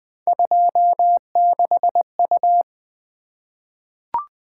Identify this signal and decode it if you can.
Morse code, heard as a keyed tone, about 20 wpm, 700 Hz. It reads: 26U
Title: Three Letter-Number Combinations - 20wpm